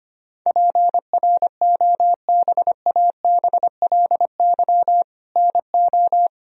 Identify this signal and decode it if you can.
Morse code, heard as a keyed tone, about 25 wpm, 700 Hz. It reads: PROBABLY NO
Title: Sentences from Top 1000 Words - Review Part 4 - 25wpm